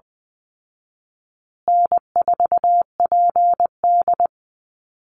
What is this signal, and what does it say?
Morse code, heard as a keyed tone, about 20 wpm, 700 Hz. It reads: N4PD
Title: Easy Call Signs Repeated 3x - 20wpm